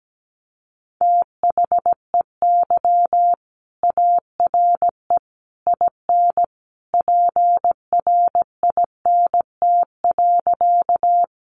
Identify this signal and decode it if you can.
Morse code, heard as a keyed tone, about 17 wpm, 700 Hz. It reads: THEY ARE IN PRINT.
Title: Sentences from Top 1400 Words 17wpm